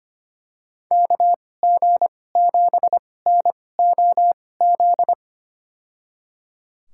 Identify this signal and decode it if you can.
Morse code, heard as a keyed tone, about 25 wpm, spaced as for 18 wpm, 700 Hz. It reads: KG7NOZ
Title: USA Call Signs - 6 Characters - 2x Character Spacing - 25wpm